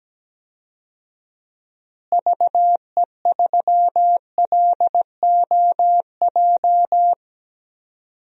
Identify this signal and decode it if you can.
Morse code, heard as a keyed tone, about 17 wpm, 700 Hz. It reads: VE3LOJ
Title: DXCC-100 Call Signs - 6 Characters - Repeated 3x - 17wpm